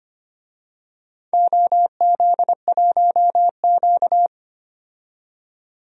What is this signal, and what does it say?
Morse code, heard as a keyed tone, about 25 wpm, 700 Hz. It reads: OZ1Q
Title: DXCC-100 Call Signs - 4 Characters - 25wpm